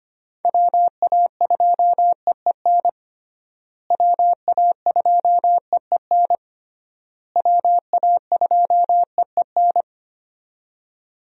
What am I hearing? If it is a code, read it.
Morse code, heard as a keyed tone, about 25 wpm, 700 Hz. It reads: WA2EEN WA2EEN WA2EEN
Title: USA Call Signs - 6 Characters - Repeated 3x - 25wpm